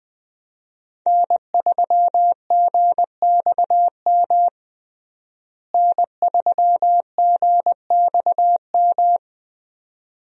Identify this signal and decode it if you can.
Morse code, heard as a keyed tone, about 20 wpm, 700 Hz. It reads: N3GXM N3GXM
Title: USA Call Signs - 5 Characters - Repeated 2x - 20wpm